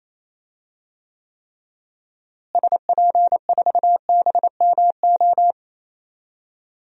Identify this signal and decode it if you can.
Morse code, heard as a keyed tone, about 28 wpm, 700 Hz. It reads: SP4BMO
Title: DXCC-100 Call Signs - 6 Characters - Repeated 3x - 28wpm